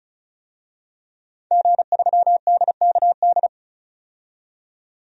Morse code, heard as a keyed tone, about 34 wpm, 700 Hz. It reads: G3DKD
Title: DXCC-100 Call Signs - 5 Characters - 35wpm